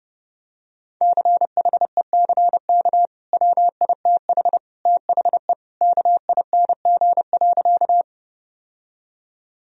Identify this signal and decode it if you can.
Morse code, heard as a keyed tone, about 29 wpm, 700 Hz. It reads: CHECK WITH THE KING.
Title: Sentences from Top 1000 Words - Review Part 3 - 30wpm